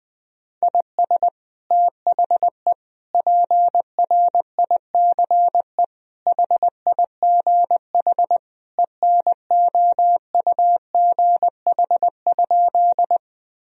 Morse code, heard as a keyed tone, about 20 wpm, 700 Hz. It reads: IS THE PRICE HIGH ENOUGH?
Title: Sentences from Top 1000 Words - Review Part 4 - 20wpm